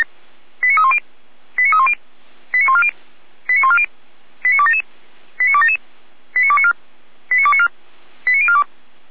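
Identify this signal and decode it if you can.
Five-tone selective call (ZVEI), heard as quick runs of five tones, tones of about 70 ms. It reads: 89310 89310 89140 89140 89277 89277 89294 89294 89042
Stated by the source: ZVEI five-tone selective call received on an SDR.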